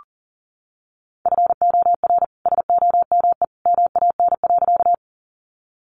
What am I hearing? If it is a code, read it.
Morse code, heard as a keyed tone, about 40 wpm, 700 Hz. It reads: FOR SOME MAN.
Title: Sentences from Top 200 Words 40wpm